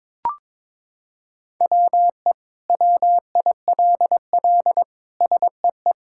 Morse code, heard as a keyed tone, about 21 wpm, 700 Hz. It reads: WE WILL SEE
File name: Sentences with 7-Word Phrases - 22wpm